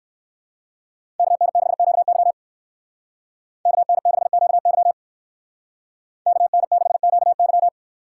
Morse code, heard as a keyed tone, about 58 wpm, 700 Hz. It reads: KN6QY KN6QY KN6QY